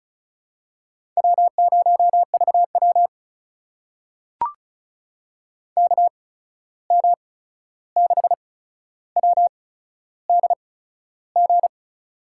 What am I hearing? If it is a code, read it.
Morse code, heard as a keyed tone, about 34 wpm, spaced as for 13 wpm, 700 Hz. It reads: W0VW KM6WDG
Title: USA Call Signs - 8x Character Spacing - 35wpm